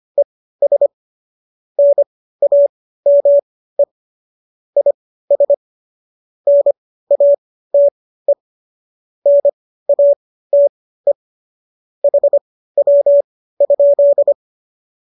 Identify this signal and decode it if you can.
Morse code, heard as a keyed tone, about 25 wpm, spaced as for 15 wpm, 570 Hz. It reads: ES NAME IS NATE NATE HW?
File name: qso211 15